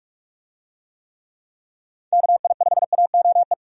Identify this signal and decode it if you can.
Morse code, heard as a keyed tone, about 46 wpm, 700 Hz. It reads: KI5AOE
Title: USA Call Signs - 6 Characters - 45wpm